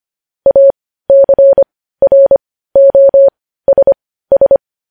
Morse code, heard as a keyed tone, about 25 wpm, spaced as for 15 wpm, 550 Hz. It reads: ACROSS
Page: news headlines in morse code at 15 wpm